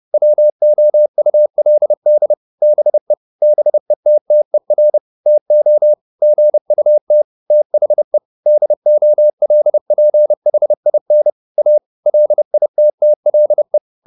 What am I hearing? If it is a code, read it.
Morse code, heard as a keyed tone, about 30 wpm, 600 Hz. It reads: WOULD BE BETTER TO GUT THE DOLPHIN A LITTLE